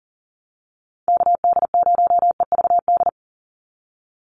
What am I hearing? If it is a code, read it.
Morse code, heard as a keyed tone, about 40 wpm, 700 Hz. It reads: KD0EVD
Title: USA Call Signs 40wpm